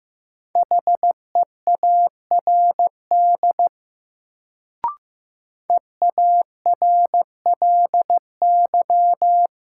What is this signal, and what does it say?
Morse code, heard as a keyed tone, about 15 wpm, 700 Hz. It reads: HEARD EARLY
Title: Sentences from Top 400 Words - New Word Drill - 15wpm